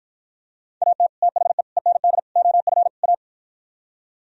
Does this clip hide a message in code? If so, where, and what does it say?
Yes: Morse code from 0.8 to 3.1 s, heard as a keyed tone, about 52 wpm, 700 Hz. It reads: AT THE END OF A